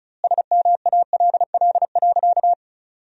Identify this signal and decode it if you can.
Morse code, heard as a keyed tone, about 36 wpm, 700 Hz. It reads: SMALL.
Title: 2x Word-Spacing - Sentences from Top 400 Words - 35wpm